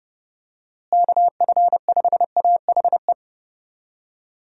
Morse code, heard as a keyed tone, about 32 wpm, 700 Hz. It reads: KF5AHE
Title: USA Call Signs - 6 Characters - Repeated 2x - 30wpm